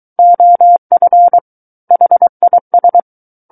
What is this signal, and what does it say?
Morse code, heard as a keyed tone, about 23 wpm, 700 Hz. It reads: OF HIS
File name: Book One / 14-Chapter 14 - 23 WPM